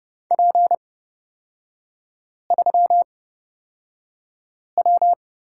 Morse code, heard as a keyed tone, about 30 wpm, spaced as for 5 wpm, 700 Hz. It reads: P3W
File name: Lesson 36e - BT - Three Characters 30wpm